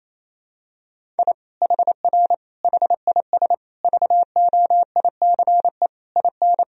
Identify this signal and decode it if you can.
Morse code, heard as a keyed tone, about 28 wpm, 700 Hz. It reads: I HR HIS VOICE IN
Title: Sentences from Top 500 Words and Abbreviations 28wpm